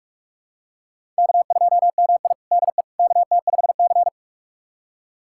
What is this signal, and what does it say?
Morse code, heard as a keyed tone, about 41 wpm, 700 Hz. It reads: K2MI DE KT5C
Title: USA Callsign DE Callsign - 4 Characters - 45wpm